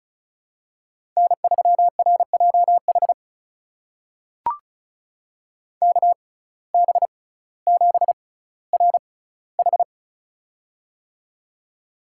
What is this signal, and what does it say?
Morse code, heard as a keyed tone, about 34 wpm, spaced as for 18 wpm, 700 Hz. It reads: N3RJH KB7RH
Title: USA Call Signs - 5 Characters - 6x Character Spacing - 35wpm